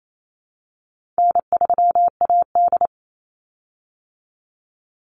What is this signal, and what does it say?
Morse code, heard as a keyed tone, about 28 wpm, 700 Hz. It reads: N3AD
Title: Call Signs - 28wpm